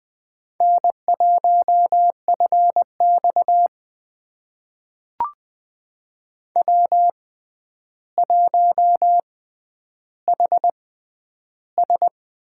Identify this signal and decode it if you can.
Morse code, heard as a keyed tone, about 20 wpm, spaced as for 10 wpm, 700 Hz. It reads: N1FX W1HS
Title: USA Call Signs - 4 Characters - 6x Character Spacing - 20wpm